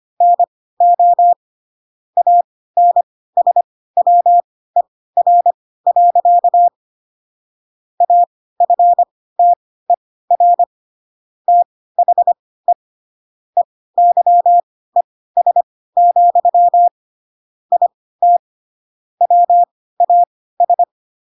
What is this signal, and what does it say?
Morse code, heard as a keyed tone, about 25 wpm, spaced as for 16 wpm, 700 Hz. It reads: NO ANSWER. AFTER THE EYES, IT WAS
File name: Pinocchio / Pinocchio ch3 10 25 ik1hge